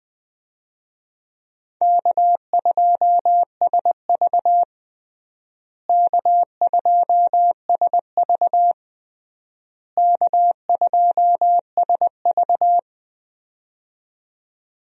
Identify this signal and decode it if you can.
Morse code, heard as a keyed tone, about 20 wpm, 700 Hz. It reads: K2SV K2SV K2SV
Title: USA Call Signs - 4 Characters - Repeated 3x - 20wpm